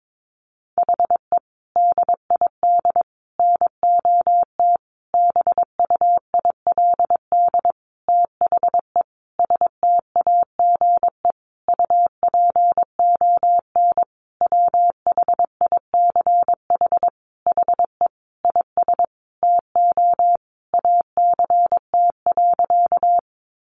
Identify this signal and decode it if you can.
Morse code, heard as a keyed tone, about 22 wpm, 700 Hz. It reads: HE DID NOT BUILD THE STAGE UPON WHICH HE IS TO ACT.
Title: Sentences from Top 1200 Words 22wpm